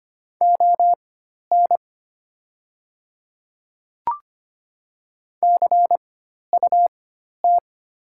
Morse code, heard as a keyed tone, about 25 wpm, spaced as for 12 wpm, 700 Hz. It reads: ON CUT
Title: 4x Character Spacing - Top 500 Words - 25wpm